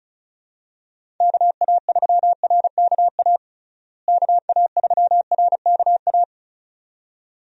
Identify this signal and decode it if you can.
Morse code, heard as a keyed tone, about 36 wpm, 700 Hz. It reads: KA3RKA KA3RKA
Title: USA Call Signs - 6 Characters - Repeated 2x - 35wpm